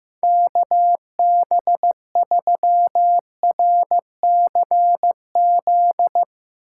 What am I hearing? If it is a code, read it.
Morse code, heard as a keyed tone, about 16 wpm, 700 Hz. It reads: KB3RCZ
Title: Speed Racing - USA 2x3 Call Signs 15wpm